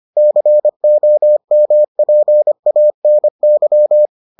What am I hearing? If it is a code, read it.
Morse code, heard as a keyed tone, about 25 wpm, 600 Hz. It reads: COMPANY